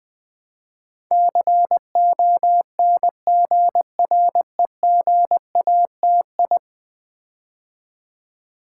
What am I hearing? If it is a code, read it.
Morse code, heard as a keyed tone, about 20 wpm, 700 Hz. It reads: CONGREGATI
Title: Build A Word - 5 Letters At A Time - 12-Letter Words - 20wpm